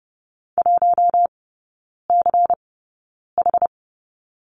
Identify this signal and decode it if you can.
Morse code, heard as a keyed tone, about 30 wpm, spaced as for 9 wpm, 700 Hz. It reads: 1CH
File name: Lesson 16e - Letter c - Three Characters 30wpm